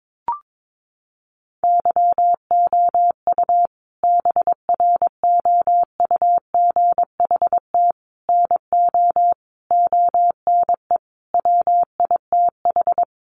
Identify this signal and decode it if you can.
Morse code, heard as a keyed tone, about 22 wpm, 700 Hz. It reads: YOU BROUGHT NO ONE WITH
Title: Sentences from Top 500 Words 22wpm